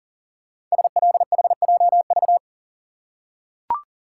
Morse code, heard as a keyed tone, about 41 wpm, 700 Hz. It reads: SLHJV
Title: Code Groups - 5 Letters - 40wpm